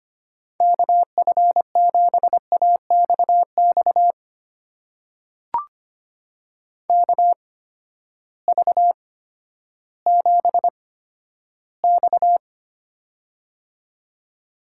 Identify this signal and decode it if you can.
Morse code, heard as a keyed tone, about 25 wpm, spaced as for 12 wpm, 700 Hz. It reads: KF7AXX KV7X
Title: USA Call Signs - 8x Character Spacing - 25wpm